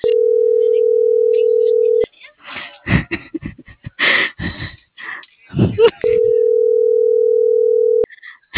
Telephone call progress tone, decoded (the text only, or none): ringback tone